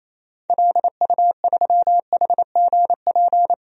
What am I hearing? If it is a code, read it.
Morse code, heard as a keyed tone, about 28 wpm, 700 Hz. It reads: LU3HGP
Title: DXCC-100 Call Signs - 6 Characters - 28wpm